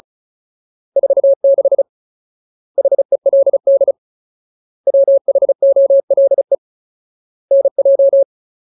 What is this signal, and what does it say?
Morse code, heard as a keyed tone, about 34 wpm, 550 Hz. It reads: 46 HELD WHOLE NJ
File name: Ultimate Word List / 35WPM Ultimate